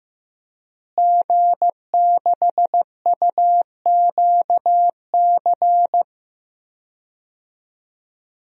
Morse code, heard as a keyed tone, about 15 wpm, 700 Hz. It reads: G6UQC
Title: DXCC-100 Call Signs - 5 Characters - Repeated 2x - 15wpm